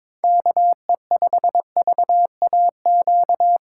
Morse code, heard as a keyed tone, about 22 wpm, 700 Hz. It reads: KE5VAQ
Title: USA Call Signs - 6 Characters - 3x Character Spacing - 22wpm